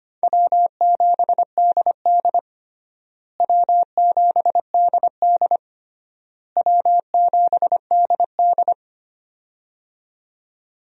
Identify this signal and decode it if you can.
Morse code, heard as a keyed tone, about 25 wpm, 700 Hz. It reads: W7DD W7DD W7DD
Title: USA Call Signs - 4 Characters - Repeated 3x - 25wpm